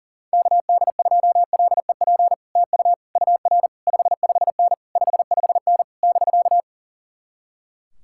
Morse code, heard as a keyed tone, about 49 wpm, 700 Hz. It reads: KD2LEP TU UR 55N 55N <BK>
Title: Speed-Racing Common QSO Phrases - 40wpm